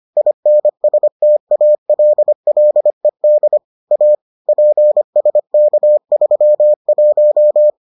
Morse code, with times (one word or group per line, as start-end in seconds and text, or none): 0.2-3.6 INSTALLED
3.9-4.2 A
4.5-7.7 PSK31